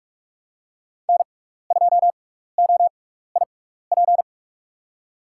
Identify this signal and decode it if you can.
Morse code, heard as a keyed tone, about 42 wpm, spaced as for 15 wpm, 700 Hz. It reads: N2OIP
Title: USA Call Signs - 5 Characters - 6x Character Spacing - 45wpm